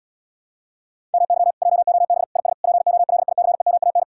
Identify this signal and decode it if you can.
Morse code, heard as a keyed tone, about 80 wpm, 700 Hz. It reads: MY JOB IS POLICEMAN